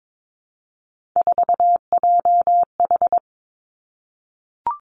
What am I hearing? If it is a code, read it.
Morse code, heard as a keyed tone, about 22 wpm, 700 Hz. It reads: 4JH
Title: Three Challenging Letter-Number Combinations 22wpm